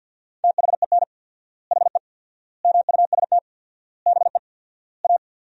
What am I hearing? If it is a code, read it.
Morse code, heard as a keyed tone, about 50 wpm, 700 Hz. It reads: THEN HE MUST BE A
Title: 4x Word-Spacing - Sentences from Top 500 Words - 50wpm